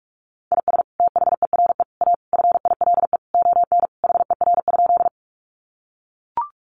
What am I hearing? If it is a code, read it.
Morse code, heard as a keyed tone, about 44 wpm, 700 Hz. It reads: IS THERE A FILE ON HER?
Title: Sentences from Top 1200 Words 45wpm